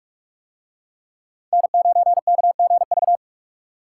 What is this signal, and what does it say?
Morse code, heard as a keyed tone, about 42 wpm, 700 Hz. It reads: N9KGV